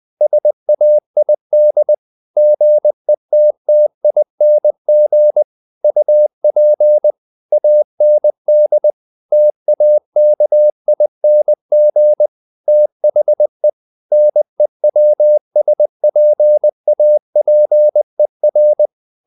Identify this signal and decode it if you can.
Morse code, heard as a keyed tone, about 20 wpm, 600 Hz. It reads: SAID GETTING UP AND TAKING THE NEWSPAPER
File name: the old man and the sea / the old man and the sea part 1 20wpm